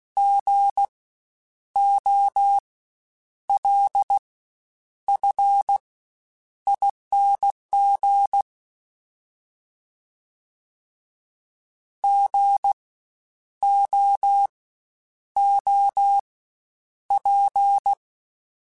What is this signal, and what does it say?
Morse code, heard as a keyed tone, about 16 wpm, spaced as for 8 wpm, 800 Hz. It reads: GOLFING GOOP
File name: Spaced Alphas / Spaced Alphas mp3s / Gs sp @15